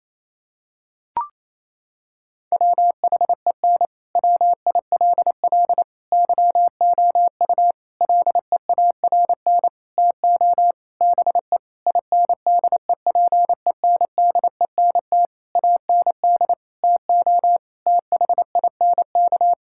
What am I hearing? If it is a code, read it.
Morse code, heard as a keyed tone, about 27 wpm, 700 Hz. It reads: WHEN WILL YOU LEARN TO BE INDEPENDENT AND TO THINK
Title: Sentences from Top 1000 Words - Review Part 6 - 28wpm